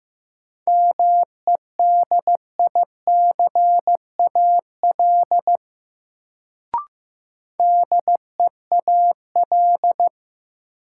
Morse code, heard as a keyed tone, about 15 wpm, 700 Hz. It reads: MEDICAL DEAL
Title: Sentences from Top 800 Words - New Word Drill - 15wpm